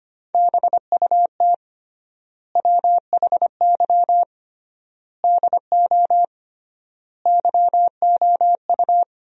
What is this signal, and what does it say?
Morse code, heard as a keyed tone, about 25 wpm, 700 Hz. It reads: BUT WHY DO YOU
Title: 3x Word-Spacing - Sentences from Top 200 Words - 25wpm